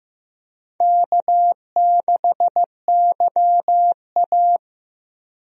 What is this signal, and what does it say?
Morse code, heard as a keyed tone, about 15 wpm, 700 Hz. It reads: K6YA
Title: USA Call Signs - 4 Characters - Repeated 2x - 15wpm